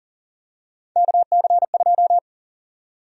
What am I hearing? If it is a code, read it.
Morse code, heard as a keyed tone, about 40 wpm, 700 Hz. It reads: KC2
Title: USA Call Sign Prefixes - 40wpm